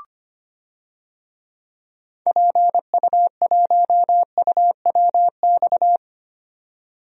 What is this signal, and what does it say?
Morse code, heard as a keyed tone, about 25 wpm, 700 Hz. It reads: PU1UWX
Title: DXCC-100 Call Signs - 6 Characters - 25wpm